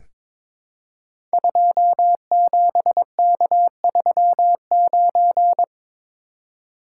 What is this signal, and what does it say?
Morse code, heard as a keyed tone, about 22 wpm, 700 Hz. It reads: 27K39
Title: Code Groups - 5 Letter:Numbers - 22wpm